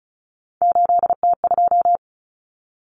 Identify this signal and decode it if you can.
Morse code, heard as a keyed tone, about 34 wpm, 700 Hz. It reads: 8T2
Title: Common Cut Numbers - 3-Digit 35wpm